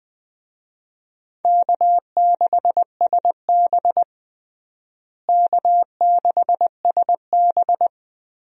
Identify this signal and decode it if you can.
Morse code, heard as a keyed tone, about 21 wpm, 700 Hz. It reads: K6SB K6SB
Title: USA Call Signs - 4 Characters - Repeated 3x - 20wpm